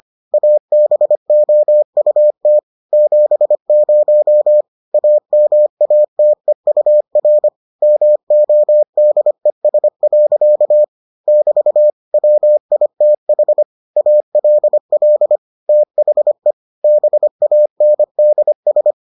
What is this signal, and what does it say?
Morse code, heard as a keyed tone, about 25 wpm, 600 Hz. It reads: ABOUT 70 AMATEUR MODES. <BT> WITH ALL THE BANDS